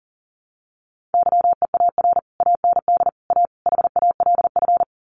Morse code, heard as a keyed tone, about 40 wpm, 700 Hz. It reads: YEAR AND A HALF
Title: Sets of 4 Words 40wpm